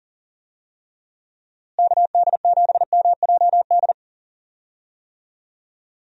Morse code, heard as a keyed tone, about 37 wpm, 700 Hz. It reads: KD7MJD
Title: USA Call Signs - 6 Characters - Repeated 3x - 40wpm